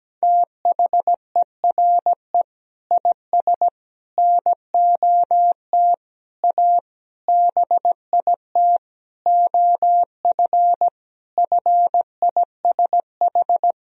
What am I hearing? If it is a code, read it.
Morse code, heard as a keyed tone, about 17 wpm, 700 Hz. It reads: THERE IS NOT A BIT OF FISH
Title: Sentences from Top 1000 Words - Review Part 1 - 17wpm